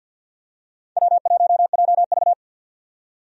Morse code, heard as a keyed tone, about 46 wpm, 700 Hz. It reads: W1JV
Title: USA Call Signs - 4 Characters - 50wpm